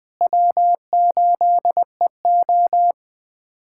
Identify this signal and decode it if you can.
Morse code, heard as a keyed tone, about 21 wpm, 700 Hz. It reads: W8EO
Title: USA Callsign DE Callsign - 4 Characters - 20wpm